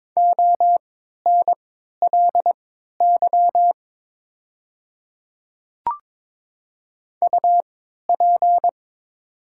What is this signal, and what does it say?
Morse code, heard as a keyed tone, about 22 wpm, spaced as for 12 wpm, 700 Hz. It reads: ONLY UP
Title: 3x Character Spacing - Top 300 Words - 22wpm